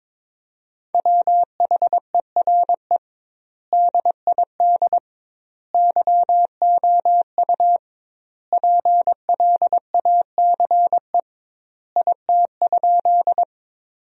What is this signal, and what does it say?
Morse code, heard as a keyed tone, about 22 wpm, 700 Hz. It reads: WHERE DID YOU PLACE IT?
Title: 2x Word-Spacing - Sentences from Top 200 Words - 22wpm